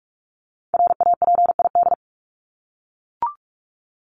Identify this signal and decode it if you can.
Morse code, heard as a keyed tone, about 44 wpm, 700 Hz. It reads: RAPID
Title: LY Suffix - Top 100 Words with Base Words 45wpm